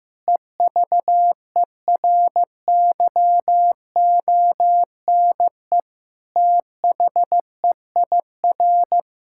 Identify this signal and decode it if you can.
Morse code, heard as a keyed tone, about 15 wpm, 700 Hz. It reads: EVERYONE THEIR
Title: Sentences from Top 1000 Words - Review Part 6 - 15wpm